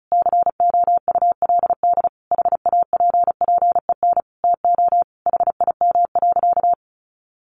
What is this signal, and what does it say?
Morse code, heard as a keyed tone, about 34 wpm, 700 Hz. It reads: COULD HAPPEN TO HIM.
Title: Sentences from Top 400 Words 35wpm